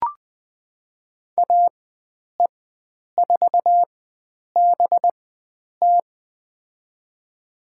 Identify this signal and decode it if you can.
Morse code, heard as a keyed tone, about 20 wpm, spaced as for 9 wpm, 700 Hz. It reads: AE4BT
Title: USA Call Signs - 5 Characters - 4x Character Spacing - 20wpm